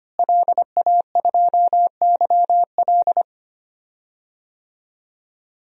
Morse code, heard as a keyed tone, about 25 wpm, 700 Hz. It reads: LA2YL
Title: DXCC-100 Call Signs - 5 Characters - Repeated 2x - 25wpm